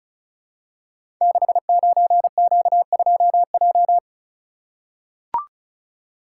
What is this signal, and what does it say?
Morse code, heard as a keyed tone, about 36 wpm, 700 Hz. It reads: 69Q2J
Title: Code Groups - 5 Letter:Numbers - 35wpm